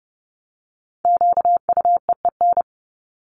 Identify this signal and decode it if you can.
Morse code, heard as a keyed tone, about 30 wpm, 700 Hz. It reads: QUEEN
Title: Top 1751-2000 Words 30wpm